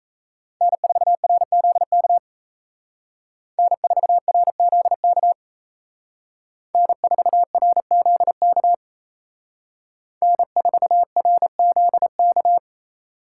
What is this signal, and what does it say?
Morse code, heard as a keyed tone, about 36 wpm, 700 Hz. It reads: N4RZK N4RZK N4RZK N4RZK